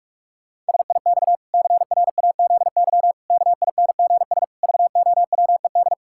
Morse code, heard as a keyed tone, about 41 wpm, 700 Hz. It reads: SIX CRAZY KINGS VOWED
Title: Pangram Challenge v2 - 45wpm